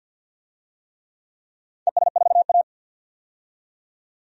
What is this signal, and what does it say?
Morse code, heard as a keyed tone, about 51 wpm, 700 Hz. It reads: ES4A
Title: DXCC-100 Call Signs - 4 Characters - 50wpm